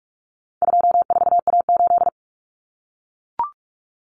Morse code, heard as a keyed tone, about 44 wpm, 700 Hz. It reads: 24A8
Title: Common Cut Numbers - 4-Digit 45wpm